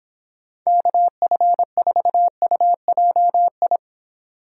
Morse code, heard as a keyed tone, about 29 wpm, 700 Hz. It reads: KF4UJI